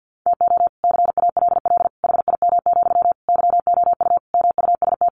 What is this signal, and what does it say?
Morse code, heard as a keyed tone, about 50 wpm, 700 Hz. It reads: TO CALL HIM, YOU MUST
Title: Sentences from Top 1400 Words 50wpm